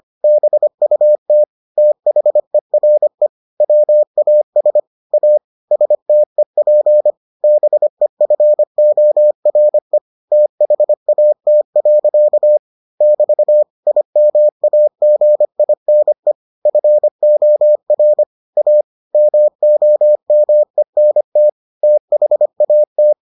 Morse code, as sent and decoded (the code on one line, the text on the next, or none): -... ..- - / - .... .-. / .-- .- ... / .- / ... - . .--. / -... ..-. --- .-. / - .... .- - .-.-.- / -...- / .. -- .- --. .. -. / ..-. --- .-. / .- / -- --- -- . -. - / - .... .- -
BUT THERE WAS A STEP BEFORE THAT. <BT> IMAGINE FOR A MOMENT THAT